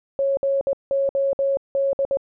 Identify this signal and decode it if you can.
Morse code, heard as a keyed tone, about 20 wpm, 550 Hz. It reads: GOD